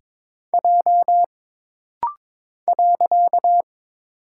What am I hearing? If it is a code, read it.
Morse code, heard as a keyed tone, about 22 wpm, 700 Hz. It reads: J .
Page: This rapid-fire timing is geared to push your ICR (Instant Character Recognition) skills.